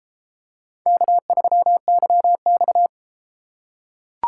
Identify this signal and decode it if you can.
Morse code, heard as a keyed tone, about 31 wpm, 700 Hz. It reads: K3YX